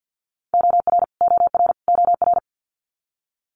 Morse code, heard as a keyed tone, about 50 wpm, 700 Hz. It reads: OR OR OR
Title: Multi-letter Phonemes Repeated 3x 50wpm